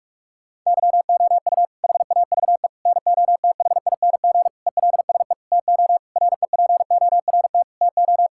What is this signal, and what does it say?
Morse code, heard as a keyed tone, about 46 wpm, 700 Hz. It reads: YOU HAVE NOTHING ELSE TO REPORT TO
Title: Sentences from Top 1000 Words - Review Part 6 - 45wpm